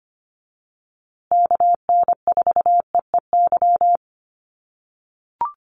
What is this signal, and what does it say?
Morse code, heard as a keyed tone, about 25 wpm, 700 Hz. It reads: KN4EEY
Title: USA Call Signs Repeated 3x 25wpm